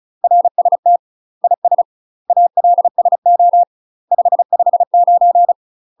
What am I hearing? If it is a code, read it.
Morse code, heard as a keyed tone, about 35 wpm, 700 Hz. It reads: RST IS ALSO 559
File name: UK MASTER QSOs TRAINING / 35WPM / UK MASTER QSOs 0032